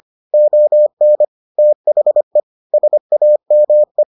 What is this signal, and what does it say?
Morse code, heard as a keyed tone, about 25 wpm, 600 Hz. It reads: ON THE SAME